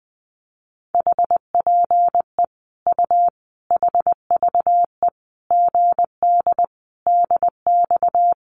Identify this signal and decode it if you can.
Morse code, heard as a keyed tone, about 19 wpm, 700 Hz. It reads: HPE U HVE GD DX
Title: Common QSO Phrases 20wpm